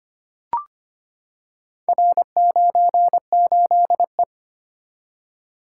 Morse code, heard as a keyed tone, about 25 wpm, 700 Hz. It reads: R98E
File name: practice / 25wpm / Four Letter-Number Combinations - 25wpm